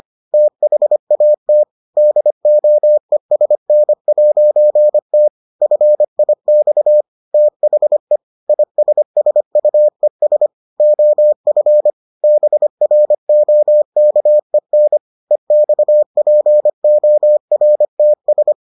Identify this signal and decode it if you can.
Morse code, heard as a keyed tone, about 25 wpm, 600 Hz. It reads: THAT DOESN'T FIX THE ISSUES OF BROKEN EXPORTS